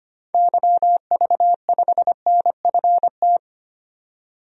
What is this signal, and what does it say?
Morse code, heard as a keyed tone, about 25 wpm, 700 Hz. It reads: YV5NFT
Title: DXCC-100 Call Signs - 6 Characters - 25wpm